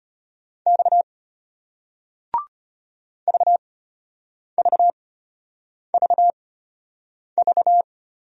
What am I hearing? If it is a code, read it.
Morse code, heard as a keyed tone, about 33 wpm, spaced as for 8 wpm, 700 Hz. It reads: X VVVV